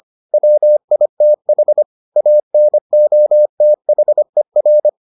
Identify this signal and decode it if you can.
Morse code, heard as a keyed tone, about 25 wpm, 600 Hz. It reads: WITH ANOTHER